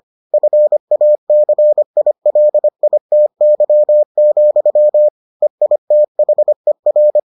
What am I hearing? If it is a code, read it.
Morse code, heard as a keyed tone, about 25 wpm, 600 Hz. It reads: FACILITY, EITHER